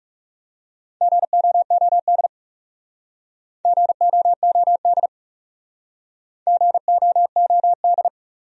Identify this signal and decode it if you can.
Morse code, heard as a keyed tone, about 38 wpm, 700 Hz. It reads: GOOD GOOD GOOD